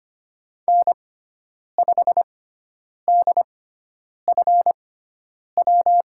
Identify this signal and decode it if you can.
Morse code, heard as a keyed tone, about 25 wpm, spaced as for 9 wpm, 700 Hz. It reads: N5DFW
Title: USA Call Signs - 5 Characters - 6x Character Spacing - 25wpm